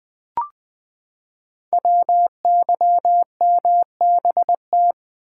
Morse code, heard as a keyed tone, about 20 wpm, 700 Hz. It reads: WYMBT
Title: Code Groups - 5 Letters - 20wpm